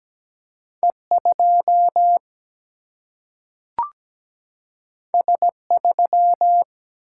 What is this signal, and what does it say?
Morse code, heard as a keyed tone, about 17 wpm, 700 Hz. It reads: E 2 S 3
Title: Single-Letter Single-Number 17wpm